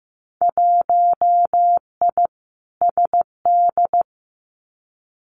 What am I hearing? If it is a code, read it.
Morse code, heard as a keyed tone, about 15 wpm, 700 Hz. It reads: 1I SD
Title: Winter Field Day Exchanges 15wpm